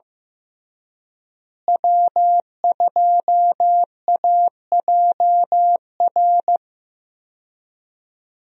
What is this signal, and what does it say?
Morse code, heard as a keyed tone, about 16 wpm, 700 Hz. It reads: W2AJR
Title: USA Call Signs - 5 Characters - Repeated 3x - 15wpm